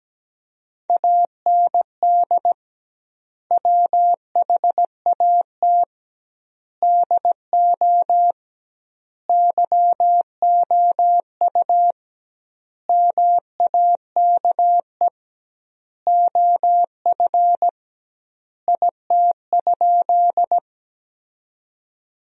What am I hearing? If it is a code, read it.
Morse code, heard as a keyed tone, about 17 wpm, 700 Hz. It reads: AND WHAT DO YOU MAKE OF IT?
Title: 2x Word-Spacing - Sentences from Top 100 Words - 17wpm